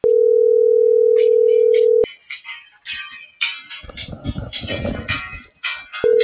Ringback tone (North America), 440 + 480 Hz.